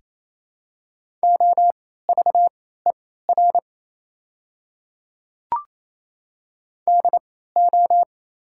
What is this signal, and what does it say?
Morse code, heard as a keyed tone, about 27 wpm, spaced as for 16 wpm, 700 Hz. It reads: OVER DO
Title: 3x Character Spacing - Top 300 Words - 28wpm